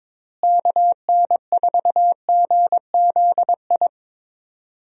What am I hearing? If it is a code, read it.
Morse code, heard as a keyed tone, about 21 wpm, 700 Hz. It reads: KN4GZI
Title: USA Call Signs - 6 Characters - Repeated 3x - 22wpm